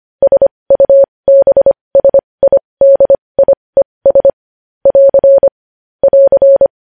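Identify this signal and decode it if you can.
Morse code, heard as a keyed tone, about 25 wpm, spaced as for 20 wpm, 550 Hz. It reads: SUBSIDIES <AR> <AR>
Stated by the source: News headlines sent in Morse code at 20.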